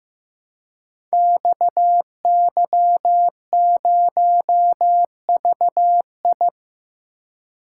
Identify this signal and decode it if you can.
Morse code, heard as a keyed tone, about 15 wpm, 700 Hz. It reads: XY0VI
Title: Code Groups - 5 Letter:Numbers - 15wpm